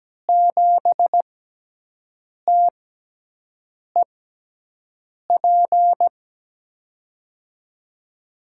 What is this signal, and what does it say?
Morse code, heard as a keyed tone, about 17 wpm, spaced as for 6 wpm, 700 Hz. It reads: 7TEP
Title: USA Call Signs - 6 Characters - 6x Character Spacing - 17wpm